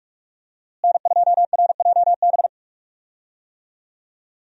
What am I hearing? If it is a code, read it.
Morse code, heard as a keyed tone, about 46 wpm, 700 Hz. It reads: N2RJB
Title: USA Call Signs - 5 Characters - Repeated 2x - 45wpm